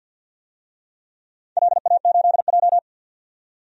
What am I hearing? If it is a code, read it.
Morse code, heard as a keyed tone, about 53 wpm, 700 Hz. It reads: LA8J
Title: DXCC-100 Call Signs - 4 Characters - 50wpm